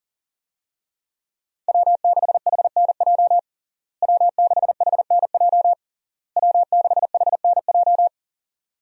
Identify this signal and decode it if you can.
Morse code, heard as a keyed tone, about 40 wpm, 700 Hz. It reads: W6HNJ W6HNJ W6HNJ
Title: USA Call Signs - 5 Characters - Repeated 3x - 40wpm